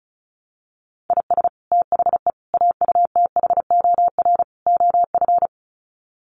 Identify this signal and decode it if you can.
Morse code, heard as a keyed tone, about 34 wpm, 700 Hz. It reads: IS THE AUTHOR OF
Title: Sets of 4 Words 35wpm